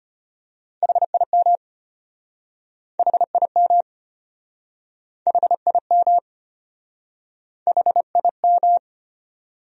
Morse code, heard as a keyed tone, about 34 wpm, 700 Hz. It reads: HIM HIM HIM HIM